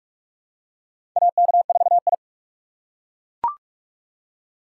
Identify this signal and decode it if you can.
Morse code, heard as a keyed tone, about 47 wpm, 690 Hz. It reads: AK4I
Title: USA Call Signs - 4 Characters - 4x Character Spacing - 45wpm